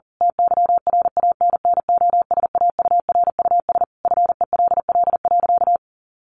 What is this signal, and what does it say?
Morse code, heard as a keyed tone, about 40 wpm, 700 Hz. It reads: TYRANNOSAURUS FELL.
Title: A Sound Of Thunder 40wpm